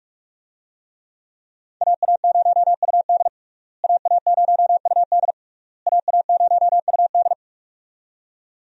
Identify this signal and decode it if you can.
Morse code, heard as a keyed tone, about 46 wpm, 700 Hz. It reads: AA0UD AA0UD AA0UD
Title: USA Call Signs - 5 Characters - Repeated 3x - 45wpm